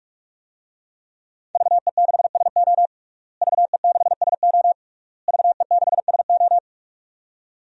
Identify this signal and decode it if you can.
Morse code, heard as a keyed tone, about 46 wpm, 700 Hz. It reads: VE6SO VE6SO VE6SO
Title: DXCC-100 Call Signs - 5 Characters - Repeated 3x - 45wpm